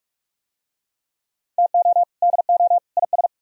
Morse code, heard as a keyed tone, about 40 wpm, 700 Hz. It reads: TO DO IS